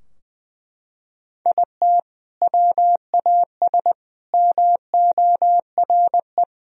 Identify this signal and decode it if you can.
Morse code, heard as a keyed tone, about 20 wpm, 700 Hz. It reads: IT WAS MORE
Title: Sentences from Top 1000 Words - Review Part 3 - 20wpm